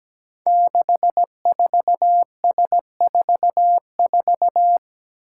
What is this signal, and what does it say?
Morse code, heard as a keyed tone, about 17 wpm, 700 Hz. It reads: 64S44
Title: Five Challenging Letter-Number Combinations 17wpm